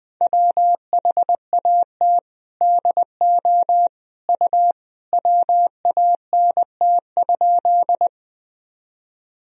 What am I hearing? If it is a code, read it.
Morse code, heard as a keyed tone, about 19 wpm, 700 Hz. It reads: WHAT DO U WANT?
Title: Sentences from Top 200 Words and Abbreviations 20wpm